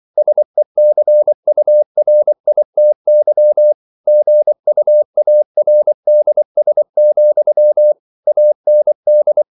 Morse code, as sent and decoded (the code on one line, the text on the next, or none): ... -.-. ..- .-. .. - -.-- / --. ..- .- .-. -.. ... --..-- / .- -. -..
SECURITY GUARDS, AND